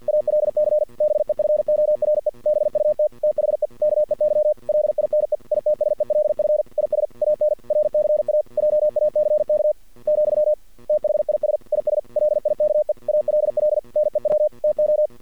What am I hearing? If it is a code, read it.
Morse code, heard as a keyed tone, about 50 wpm, 610 Hz. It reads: ALC BEFORE BUT THE KEY LINE ITSELF IS AN RCA JACK , THIS IS LIKE ALL NEW TO
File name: archived / CRSnet